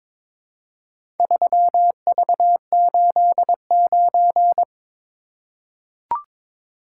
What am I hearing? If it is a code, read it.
Morse code, heard as a keyed tone, about 22 wpm, 700 Hz. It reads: 3V89
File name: practice / 22wpm / Four Letter-Number Combinations - 22wpm